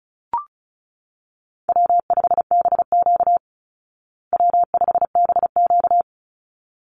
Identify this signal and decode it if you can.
Morse code, heard as a keyed tone, about 34 wpm, 700 Hz. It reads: W5BQ W5BQ
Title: Easy Call Signs Repeated 2x - 35wpm